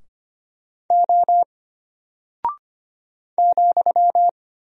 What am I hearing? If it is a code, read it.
Morse code, heard as a keyed tone, about 25 wpm, 700 Hz. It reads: O ,
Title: Single Character (Letter:Number:Punctuation:Prosign) Alt 25wpm